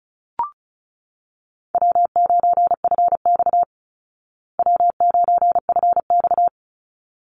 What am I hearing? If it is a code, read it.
Morse code, heard as a keyed tone, about 34 wpm, 700 Hz. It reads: W9FX W9FX
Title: Call Signs Repeated 3x - 35wpm